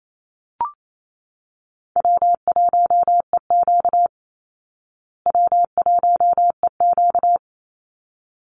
Morse code, heard as a keyed tone, about 28 wpm, 700 Hz. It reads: W1EQ W1EQ
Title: Call Signs Repeated 3x - 28wpm